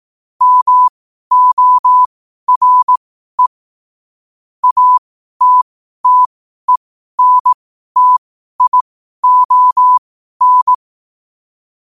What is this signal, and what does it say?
Morse code, heard as a keyed tone, about 18 wpm, spaced as for 12 wpm, 1000 Hz. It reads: MORE ATTENTION
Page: Quotes for Wed, 13 Aug 2025 in Morse Code at 12 words per minute.